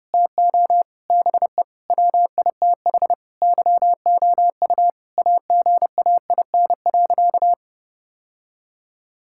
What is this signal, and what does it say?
Morse code, heard as a keyed tone, about 30 wpm, 700 Hz. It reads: TO BE WITH YOU AGAIN.
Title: Sentences from Top 1600 Words 30wpm